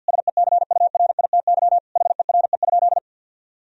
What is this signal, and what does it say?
Morse code, heard as a keyed tone, about 50 wpm, 700 Hz. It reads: SECURITY HERE?
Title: Sentences from Top 1000 Words - Review Part 4 - 50wpm